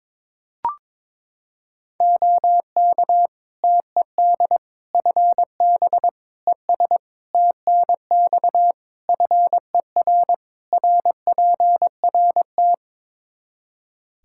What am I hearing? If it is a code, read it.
Morse code, heard as a keyed tone, about 22 wpm, 700 Hz. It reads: OK TED FB ES TNX FER RPRT
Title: Practice QSO - Exchange 2 - Info - 22wpm